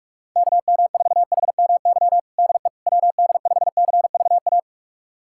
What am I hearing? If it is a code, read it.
Morse code, heard as a keyed tone, about 41 wpm, 700 Hz. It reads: KM4HMY DE WD5CVA
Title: USA Callsign DE Callsign - 6 Characters - 45wpm